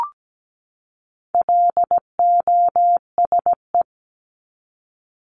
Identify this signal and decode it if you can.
Morse code, heard as a keyed tone, about 17 wpm, 700 Hz. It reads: LOSE
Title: Top 1001-1250 Words 17wpm